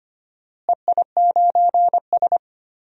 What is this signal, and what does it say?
Morse code, heard as a keyed tone, about 25 wpm, 700 Hz. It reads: EI9S
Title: DXCC-100 Call Signs - 4 Characters - Repeated 3x - 25wpm